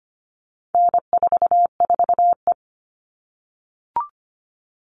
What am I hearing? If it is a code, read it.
Morse code, heard as a keyed tone, about 25 wpm, 700 Hz. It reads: N44E
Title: All Cut Numbers - 4-Digit 25wpm